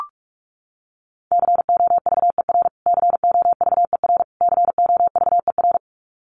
Morse code, heard as a keyed tone, about 44 wpm, 700 Hz. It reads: COVER COVER COVER
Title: Top 300 Words Repeated 45wpm